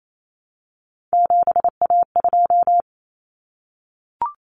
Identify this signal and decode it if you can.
Morse code, heard as a keyed tone, about 28 wpm, 700 Hz. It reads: 7A2
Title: Common Cut Numbers - 3-Digit 28wpm